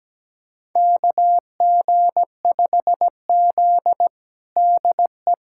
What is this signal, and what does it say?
Morse code, heard as a keyed tone, about 17 wpm, 700 Hz. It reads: KG5Z DE
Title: USA Callsign DE Callsign - 4 Characters - 17wpm